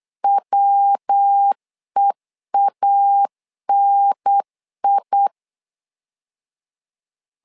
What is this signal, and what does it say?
Morse code, heard as a keyed tone, about 8 wpm, 800 Hz. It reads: WEANI
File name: LEZIONI / 3 MWRO / 3 MW 8wpm 2501